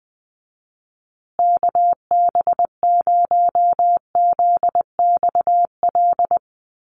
Morse code, heard as a keyed tone, about 20 wpm, 700 Hz. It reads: KB0ZXL
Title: USA Call Signs Repeated 2x 20wpm